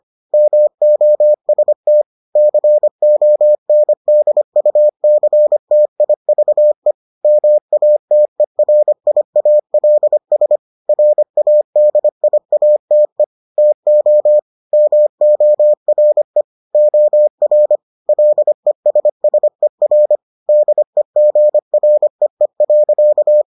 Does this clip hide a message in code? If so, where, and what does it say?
Yes: Morse code from 0.3 to 23.4 s, heard as a keyed tone, about 25 wpm, 600 Hz. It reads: MOST CONDUCTIVE MATERIALS RADIATE TO MORE OR LESSER DEGREE.